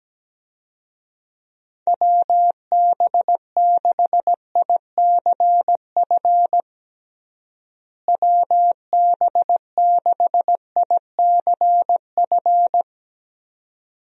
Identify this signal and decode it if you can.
Morse code, heard as a keyed tone, about 17 wpm, 700 Hz. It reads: WB6ICF WB6ICF
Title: USA Call Signs - 6 Characters - Repeated 2x - 17wpm